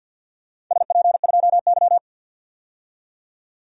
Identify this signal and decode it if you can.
Morse code, heard as a keyed tone, about 51 wpm, 700 Hz. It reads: SP2Y
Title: DXCC-100 Call Signs - 4 Characters - 50wpm